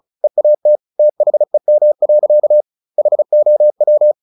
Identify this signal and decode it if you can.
Morse code, heard as a keyed tone, about 34 wpm, 600 Hz. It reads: EAT THEM. HOW
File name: the old man and the sea / the old man and the sea 1 35wpm